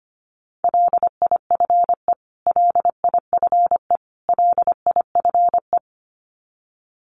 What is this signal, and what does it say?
Morse code, heard as a keyed tone, about 25 wpm, 700 Hz. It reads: LIFE LIFE LIFE
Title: Top 300 Words Repeated 25wpm